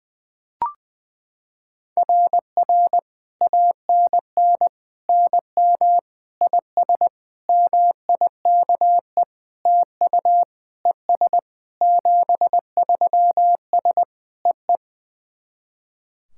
Morse code, heard as a keyed tone, about 21 wpm, 700 Hz. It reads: RR ANN NM IS MIKE TU ES 73S EE
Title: SOTA QSO Practice 20wpm